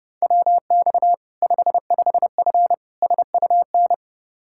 Morse code, heard as a keyed tone, about 30 wpm, 700 Hz. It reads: WX 55F SUN
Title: K1USN SST QSO Practice 30wpm